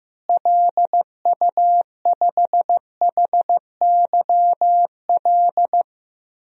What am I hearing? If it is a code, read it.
Morse code, heard as a keyed tone, about 15 wpm, 700 Hz. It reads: LU5HYL
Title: DXCC-100 Call Signs - 6 Characters - Repeated 2x - 15wpm